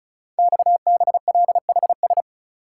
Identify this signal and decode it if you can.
Morse code, heard as a keyed tone, about 36 wpm, 690 Hz. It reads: XBLHS
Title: Code Groups - 5 Letters - 35wpm